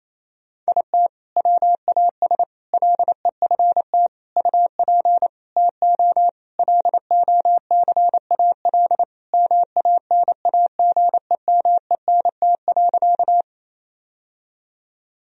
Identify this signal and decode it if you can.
Morse code, heard as a keyed tone, about 28 wpm, 700 Hz. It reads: IT WAS LEFT UP TO LOCAL MANAGEMENT.
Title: Sentences from Top 1000 Words - Review Part 4 - 28wpm